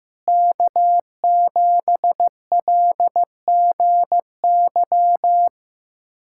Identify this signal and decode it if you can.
Morse code, heard as a keyed tone, about 15 wpm, 700 Hz. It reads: K7LGY
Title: USA Callsign DE Callsign - 5 Characters - 15wpm